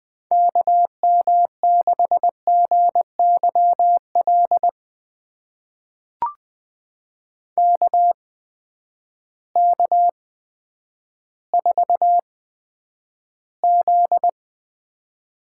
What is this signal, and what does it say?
Morse code, heard as a keyed tone, about 20 wpm, spaced as for 10 wpm, 700 Hz. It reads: KM6GYL KK4Z
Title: USA Call Signs - 6 Characters - 8x Character Spacing - 20wpm